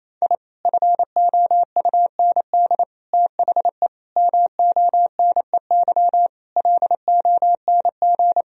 Morse code, heard as a keyed tone, about 28 wpm, 700 Hz. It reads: I FOUND THE MONEY LONG
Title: Sentences from Top 1000 Words - Review Part 3 - 28wpm